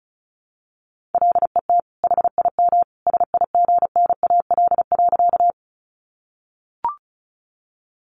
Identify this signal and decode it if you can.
Morse code, heard as a keyed tone, about 34 wpm, 700 Hz. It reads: LET HIM SIGNAL.
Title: Sentences from Top 1400 Words 35wpm